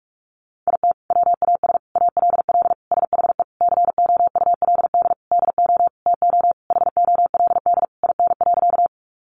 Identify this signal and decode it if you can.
Morse code, heard as a keyed tone, about 44 wpm, 700 Hz. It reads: IT WAS ALL SHE COULD DO TO HOLD IN.
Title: Sentences from Top 400 Words 45wpm